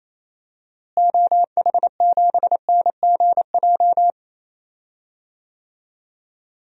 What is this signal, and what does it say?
Morse code, heard as a keyed tone, about 28 wpm, 700 Hz. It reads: OH7NGJ
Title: DXCC-100 Call Signs - 6 Characters - Repeated 3x - 28wpm